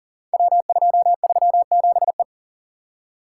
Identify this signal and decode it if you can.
Morse code, heard as a keyed tone, about 38 wpm, 700 Hz. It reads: W237E
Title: Code Groups - 5 Letter:Numbers - 40wpm